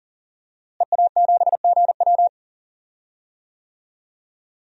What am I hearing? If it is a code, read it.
Morse code, heard as a keyed tone, about 40 wpm, 700 Hz. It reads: EA7GW
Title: DXCC-100 Call Signs - 5 Characters - Repeated 2x - 40wpm